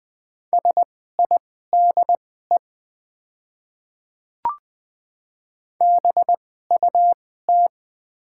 Morse code, heard as a keyed tone, about 20 wpm, spaced as for 15 wpm, 700 Hz. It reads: SIDE BUT
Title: 2x Character Spacing - Top 100 Words - 20wpm